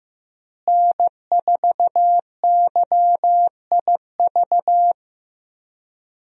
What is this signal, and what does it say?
Morse code, heard as a keyed tone, about 15 wpm, 700 Hz. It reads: N4YIV
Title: USA Call Signs - 5 Characters - Repeated 2x - 15wpm